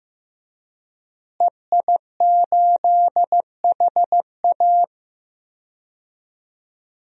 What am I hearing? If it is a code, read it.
Morse code, heard as a keyed tone, about 15 wpm, 700 Hz. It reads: EI8HA
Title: DXCC-100 Call Signs - 5 Characters - Repeated 3x - 15wpm